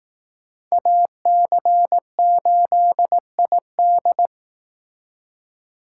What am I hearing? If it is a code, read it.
Morse code, heard as a keyed tone, about 18 wpm, 700 Hz. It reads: AC8ID
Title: Speed Racing - USA 2x2 Call Signs 15wpm